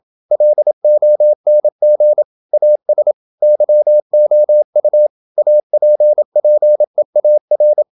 Morse code, heard as a keyed tone, about 27 wpm, 600 Hz. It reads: LONG AS YOU APPEAR
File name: Never 0027wpm